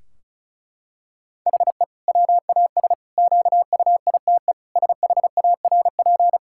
Morse code, heard as a keyed tone, about 36 wpm, 700 Hz. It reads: HE WAS QUITE SHARP
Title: Sentences from Top 1600 Words 35wpm